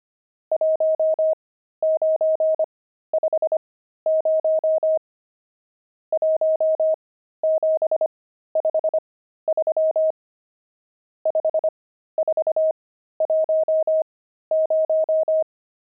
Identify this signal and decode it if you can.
Morse code, heard as a keyed tone, about 25 wpm, spaced as for 13 wpm, 630 Hz. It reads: 1950 1753 5410
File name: pangram-fs-13-cs-25-nums9